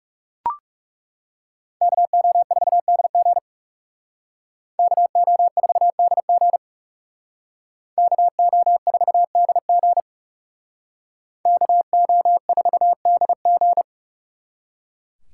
Morse code, heard as a keyed tone, about 38 wpm, 700 Hz. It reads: KO4DG KO4DG KO4DG KO4DG